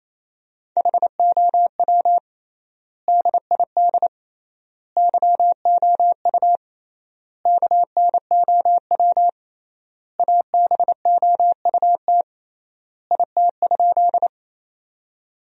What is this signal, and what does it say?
Morse code, heard as a keyed tone, about 28 wpm, 700 Hz. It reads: HOW DID YOU KNOW ABOUT IT?
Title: 3x Word-Spacing - Sentences from Top 100 Words - 28wpm